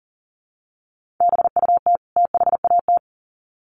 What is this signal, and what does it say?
Morse code, heard as a keyed tone, about 40 wpm, 700 Hz. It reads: BUT THAT
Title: Sets of 2 Words 40wpm